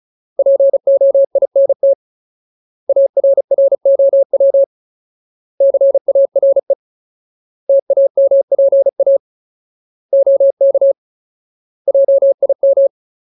Morse code, heard as a keyed tone, about 35 wpm, 550 Hz. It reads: POINT ARROW CARE TAMPA OK JIM
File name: Ultimate Word List / 35WPM Ultimate